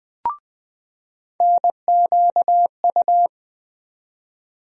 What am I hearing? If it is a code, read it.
Morse code, heard as a keyed tone, about 20 wpm, 700 Hz. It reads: NQU
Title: USA Call Sign Postfixes - 3 Letters 20wpm